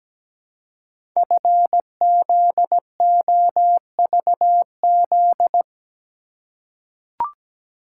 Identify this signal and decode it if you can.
Morse code, heard as a keyed tone, about 17 wpm, 700 Hz. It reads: FZOVZ
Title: Code Groups - 5 Letters - 17wpm